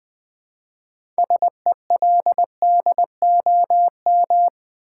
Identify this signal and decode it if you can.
Morse code, heard as a keyed tone, about 20 wpm, 700 Hz. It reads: SELDOM
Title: 6 Letter Words - First 3 Letters Given - 20wpm